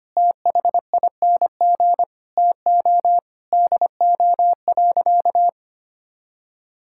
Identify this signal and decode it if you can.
Morse code, heard as a keyed tone, about 25 wpm, 700 Hz. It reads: THING TO DO.
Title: Sentences from Top 1000 Words - Review Part 5 - 25wpm